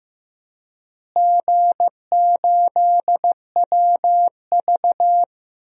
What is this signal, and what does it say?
Morse code, heard as a keyed tone, about 16 wpm, 700 Hz. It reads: G8WV
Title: DXCC-100 Call Signs - 4 Characters - Repeated 3x - 15wpm